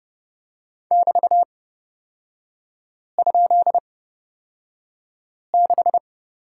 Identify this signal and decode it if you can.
Morse code, heard as a keyed tone, about 29 wpm, spaced as for 5 wpm, 700 Hz. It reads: <BT>?6
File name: Lesson 38e - Letter q - Three Characters 30wpm